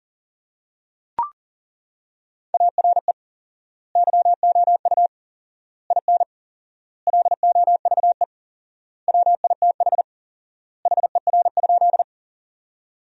Morse code, heard as a keyed tone, about 40 wpm, 700 Hz. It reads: ARE YOU IN LOVE WITH HER?
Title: 4x Word-Spacing - Sentences from Top 400 Words - 40wpm